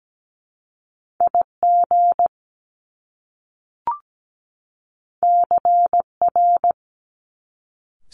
Morse code, heard as a keyed tone, about 17 wpm, 700 Hz. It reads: IG CR
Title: Top Two Letter Combinations in Words 17wpm